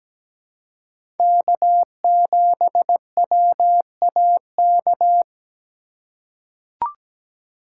Morse code, heard as a keyed tone, about 17 wpm, 700 Hz. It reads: K7WAK
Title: USA Call Signs - 5 Characters - 3x Character Spacing - 17wpm